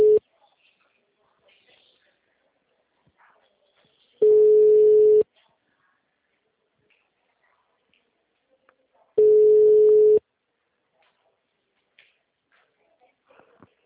Ringback tone (Europe, 425 Hz).